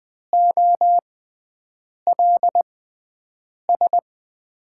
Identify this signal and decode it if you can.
Morse code, heard as a keyed tone, about 20 wpm, spaced as for 7 wpm, 700 Hz. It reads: OLS
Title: USA Call Signs - 5 Characters - 6x Character Spacing - 20wpm